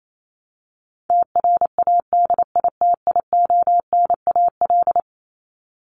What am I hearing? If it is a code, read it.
Morse code, heard as a keyed tone, about 28 wpm, 700 Hz. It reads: TRADITIONAL
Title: Top 751-1000 Words 28wpm